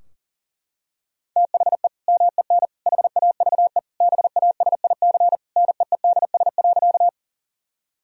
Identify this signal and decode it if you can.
Morse code, heard as a keyed tone, about 37 wpm, 700 Hz. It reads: THE MEN HAVE BASIC NEEDS.
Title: Sentences from Top 1000 Words - Review Part 3 - 40wpm